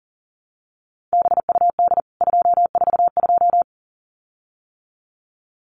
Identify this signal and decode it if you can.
Morse code, heard as a keyed tone, about 40 wpm, 700 Hz. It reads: BUD 242
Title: CWT Exchange 40wpm